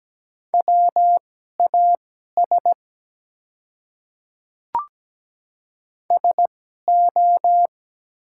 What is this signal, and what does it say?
Morse code, heard as a keyed tone, about 17 wpm, spaced as for 12 wpm, 700 Hz. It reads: WAS SO
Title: 2x Character Spacing - Top 100 Words - 17wpm